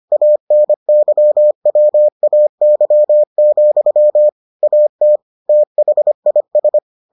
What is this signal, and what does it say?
Morse code, heard as a keyed tone, about 25 wpm, 600 Hz. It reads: ANYWAY, AT THIS